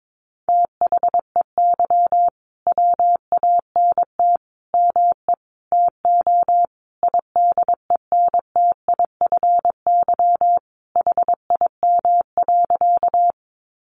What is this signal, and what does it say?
Morse code, heard as a keyed tone, about 22 wpm, 700 Hz. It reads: THEY WANT ME TO IDENTIFY HIM.
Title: Sentences from Top 900 Words 22wpm